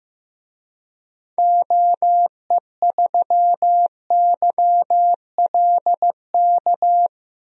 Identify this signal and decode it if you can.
Morse code, heard as a keyed tone, about 15 wpm, 700 Hz. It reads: OE3YLK
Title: DXCC-100 Call Signs - 6 Characters - Repeated 3x - 15wpm